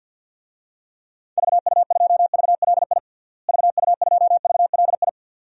Morse code, heard as a keyed tone, about 50 wpm, 700 Hz. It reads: VU2VLI VU2VLI
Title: DXCC-100 Call Signs - 6 Characters - Repeated 2x - 50wpm